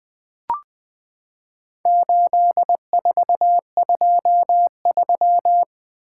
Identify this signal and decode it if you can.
Morse code, heard as a keyed tone, about 20 wpm, 700 Hz. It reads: 8423
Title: Four Letter-Number Combinations - 20wpm